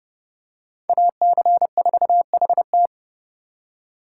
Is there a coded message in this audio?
Yes: Morse code from 0.9 to 2.9 s, heard as a keyed tone, about 28 wpm, 700 Hz. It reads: AC4HT